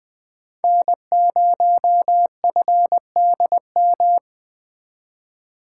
Morse code, heard as a keyed tone, about 20 wpm, 700 Hz. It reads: N0FDM